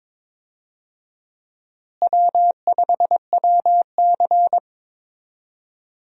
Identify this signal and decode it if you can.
Morse code, heard as a keyed tone, about 21 wpm, 700 Hz. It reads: W5WC
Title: USA Call Signs - 4 Characters - Repeated 3x - 22wpm